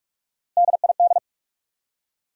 Morse code, heard as a keyed tone, about 43 wpm, 700 Hz. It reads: DID
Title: Top 100 Words 45wpm